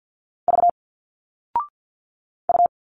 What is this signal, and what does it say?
Morse code, heard as a keyed tone, about 50 wpm, 700 Hz. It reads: V U
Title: Single Letters 50wpm